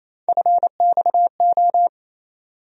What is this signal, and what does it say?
Morse code, heard as a keyed tone, about 27 wpm, 700 Hz. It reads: FXO
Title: USA Call Sign Postfixes - 3 Letters 28wpm